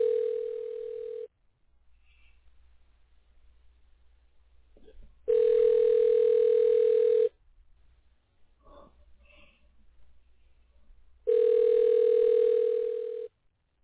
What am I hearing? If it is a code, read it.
Ringback tone (North America), 440 + 480 Hz.